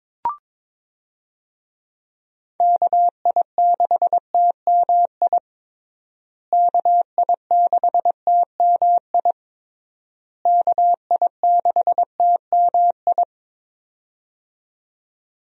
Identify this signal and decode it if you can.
Morse code, heard as a keyed tone, about 22 wpm, 700 Hz. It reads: KI6TMI KI6TMI KI6TMI
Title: USA Call Signs - 6 Characters - Repeated 3x - 22wpm